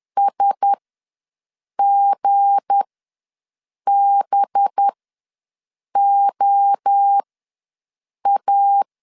Morse code, heard as a keyed tone, about 10 wpm, spaced as for 6 wpm, 800 Hz. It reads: SGBOA
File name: LEZIONI / 7 GF9Z / 7 G ws 10wpm 3843